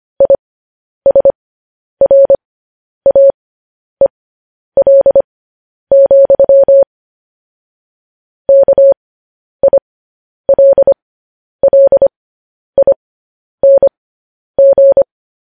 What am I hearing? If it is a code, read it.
Morse code, heard as a keyed tone, about 25 wpm, spaced as for 10 wpm, 550 Hz. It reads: ISRAEL, KILLING
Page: news-headlines-in-morse-code-at-10-wpm